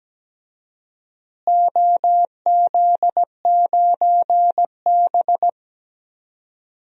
Morse code, heard as a keyed tone, about 17 wpm, 700 Hz. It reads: OZ9B
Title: DXCC-100 Call Signs - 4 Characters - Repeated 2x - 17wpm